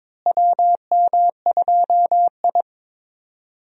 Morse code, heard as a keyed tone, about 22 wpm, 700 Hz. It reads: WM2I
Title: USA Call Signs - 4 Characters - Repeated 2x - 22wpm